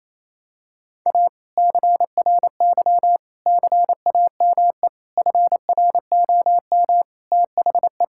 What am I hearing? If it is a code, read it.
Morse code, heard as a keyed tone, about 28 wpm, 700 Hz. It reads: A CRY CAME FROM THE
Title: Sentences from Top 1000 Words - Review Part 5 - 28wpm